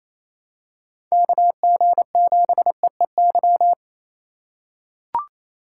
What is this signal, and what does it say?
Morse code, heard as a keyed tone, about 27 wpm, 700 Hz. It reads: KG7EEY
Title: USA Call Signs - 2x Character Spacing - 28wpm